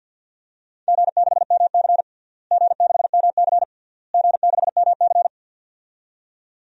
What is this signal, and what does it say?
Morse code, heard as a keyed tone, about 48 wpm, 700 Hz. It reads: G6MC G6MC G6MC
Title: DXCC-100 Call Signs - 4 Characters - Repeated 3x - 50wpm